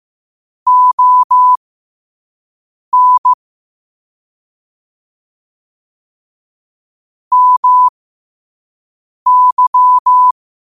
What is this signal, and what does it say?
Morse code, heard as a keyed tone, about 15 wpm, spaced as for 5 wpm, 1000 Hz. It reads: ON MY
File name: qotdfiles / 5wpm